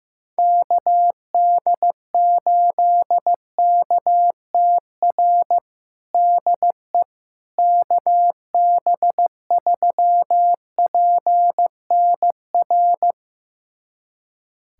Morse code, heard as a keyed tone, about 16 wpm, 700 Hz. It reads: KD8KTR DE KB3PNR
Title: USA Callsign DE Callsign - 6 Characters - 15wpm